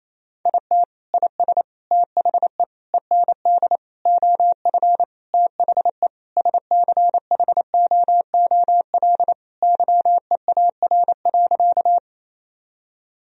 Morse code, heard as a keyed tone, about 28 wpm, 700 Hz. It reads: IT IS THE END OF THE SCHOOL YEAR.
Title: Sentences from Top 1000 Words - Review Part 1 - 28wpm